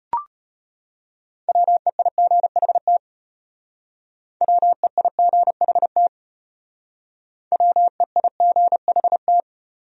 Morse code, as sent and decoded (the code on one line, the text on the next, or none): .-- . .. --. .... - / .-- . .. --. .... - / .-- . .. --. .... -
WEIGHT WEIGHT WEIGHT